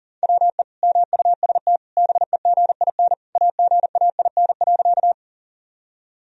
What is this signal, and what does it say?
Morse code, heard as a keyed tone, about 40 wpm, 700 Hz. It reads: WE MUST BEGIN AGAIN.
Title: Sentences from Top 1000 Words - Review Part 1 - 40wpm